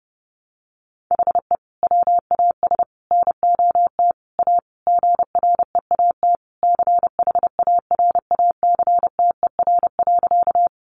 Morse code, heard as a keyed tone, about 28 wpm, 700 Hz. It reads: HE WAS NOT A GREAT CHARACTER.
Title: Sentences from Top 1200 Words 30wpm